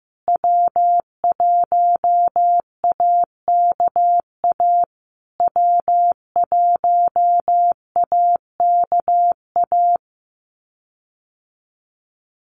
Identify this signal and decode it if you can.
Morse code, heard as a keyed tone, about 15 wpm, 700 Hz. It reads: W1AKA W1AKA
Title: USA Call Signs Repeated 3x 15wpm